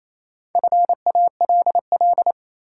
Morse code, heard as a keyed tone, about 28 wpm, 700 Hz. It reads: FALL
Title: Sentences from Top 500 Words - New Word Drill - 28wpm